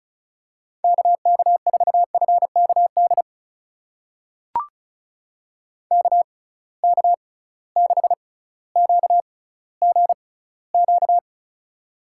Morse code, heard as a keyed tone, about 36 wpm, spaced as for 18 wpm, 700 Hz. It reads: KK4FKD KK6QGQ
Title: USA Call Signs - 6 Characters - 6x Character Spacing - 35wpm